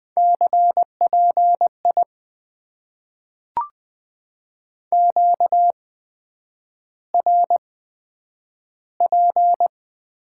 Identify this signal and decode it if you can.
Morse code, heard as a keyed tone, about 20 wpm, spaced as for 9 wpm, 700 Hz. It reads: CPI QRP
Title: Top 90 QSO Elements - 8x Character Spacing - 20wpm